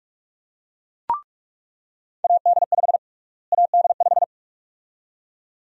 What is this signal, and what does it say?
Morse code, heard as a keyed tone, about 46 wpm, 700 Hz. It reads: AD5 AD5
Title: USA Call Sign Prefixes - Repeated 2x - 45wpm